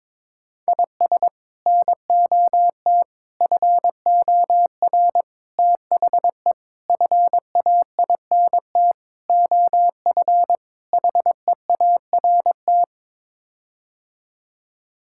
Morse code, heard as a keyed tone, about 22 wpm, 700 Hz. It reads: IS NOT FOR THE FAINT OF HEART
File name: practice / 22wpm / Sets of 7 Words - 22wpm